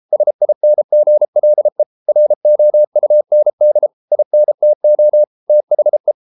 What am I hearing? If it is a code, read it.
Morse code, heard as a keyed tone, about 33 wpm, 600 Hz. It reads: SINGLE ROUND INTO THE